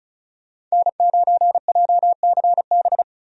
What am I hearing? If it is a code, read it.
Morse code, heard as a keyed tone, about 36 wpm, 700 Hz. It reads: N9JCB
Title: USA Call Signs - 5 Characters - Repeated 3x - 35wpm